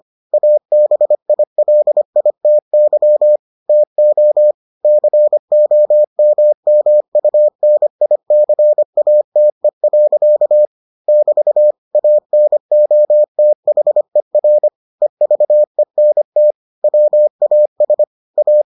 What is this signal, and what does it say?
Morse code, heard as a keyed tone, about 25 wpm, 600 Hz. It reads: ABILITY TO COMMUNICATE. <BT> ANOTHER EVENT WAS A